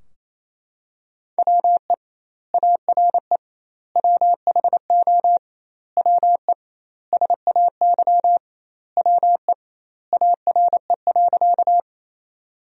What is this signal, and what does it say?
Morse code, heard as a keyed tone, about 28 wpm, 700 Hz. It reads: WE ARE WHO WE SAY WE ARE.
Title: 2x Word-Spacing - Sentences from Top 200 Words - 28wpm